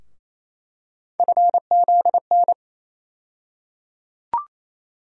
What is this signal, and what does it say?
Morse code, heard as a keyed tone, about 28 wpm, 700 Hz. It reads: FZN
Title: USA Call Sign Postfixes - 3 Letters 28wpm